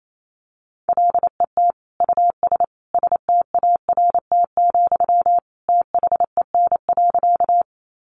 Morse code, heard as a keyed tone, about 28 wpm, 700 Hz. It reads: LET US START, THEN.
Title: Sentences from Top 300 Words 28wpm